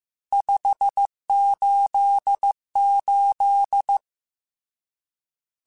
Morse code, heard as a keyed tone, about 15 wpm, 800 Hz. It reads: 588
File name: RSTs / RST-mp3s / Various RST @13